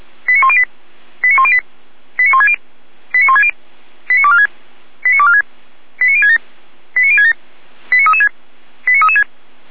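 Five-tone selective call (ZVEI), heard as quick runs of five tones, tones of about 70 ms. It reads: 89108 89108 89160 89160 89246 89246 89067 89067 89336 89336
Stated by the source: ZVEI five-tone selective call received with an SDR.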